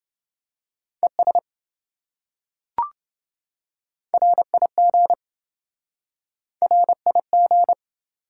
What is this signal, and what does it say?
Morse code, heard as a keyed tone, about 26 wpm, 700 Hz. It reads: ES RIG RIG